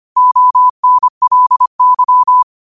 Morse code, heard as a keyed tone, about 25 wpm, 1000 Hz. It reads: ONLY